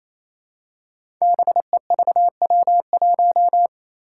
Morse code, heard as a keyed tone, about 28 wpm, 700 Hz. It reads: BEVW1
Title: Five Challenging Letter-Number Combinations 28wpm